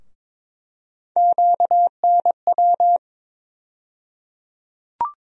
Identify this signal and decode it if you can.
Morse code, heard as a keyed tone, about 22 wpm, 700 Hz. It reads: QNW
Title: USA Call Sign Postfixes - 3 Letters 22wpm